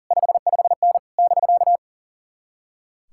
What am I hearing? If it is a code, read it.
Morse code, heard as a keyed tone, about 52 wpm, 700 Hz. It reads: 55N <BK>